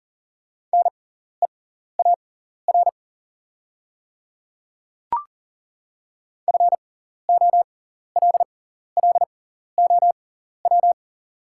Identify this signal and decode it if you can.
Morse code, heard as a keyed tone, about 40 wpm, spaced as for 14 wpm, 700 Hz. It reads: NEAR FOLLOW
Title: 6x Character Spacing - Top 200 Words - 40wpm